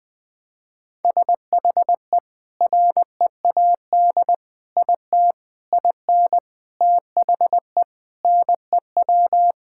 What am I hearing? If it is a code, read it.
Morse code, heard as a keyed tone, about 20 wpm, 700 Hz. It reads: SHE READ IT IN THE NEW
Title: Sentences with 5-Word Phrases - 20wpm